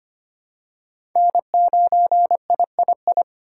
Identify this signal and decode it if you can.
Morse code, heard as a keyed tone, about 25 wpm, 700 Hz. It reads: N9III
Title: USA Call Signs - 5 Characters - 2x Character Spacing - 25wpm